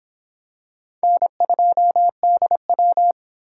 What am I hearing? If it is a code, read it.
Morse code, heard as a keyed tone, about 26 wpm, 700 Hz. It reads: N2DW